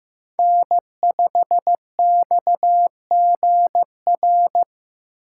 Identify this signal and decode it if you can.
Morse code, heard as a keyed tone, about 15 wpm, 700 Hz. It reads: N5XGR
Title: USA Call Signs - 5 Characters - 2x Character Spacing - 15wpm